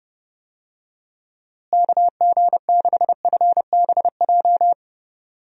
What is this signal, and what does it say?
Morse code, heard as a keyed tone, about 30 wpm, 700 Hz. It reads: KG6FBJ
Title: USA Call Signs - 6 Characters - Repeated 2x - 30wpm